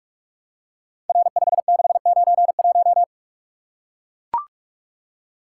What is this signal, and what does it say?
Morse code, heard as a keyed tone, about 44 wpm, 700 Hz. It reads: R5691
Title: Code Groups - 5 Letter:Numbers - 45wpm